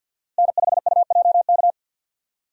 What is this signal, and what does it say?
Morse code, heard as a keyed tone, about 47 wpm, 700 Hz. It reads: N5UJK
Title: USA Call Signs - 5 Characters - Repeated 3x - 50wpm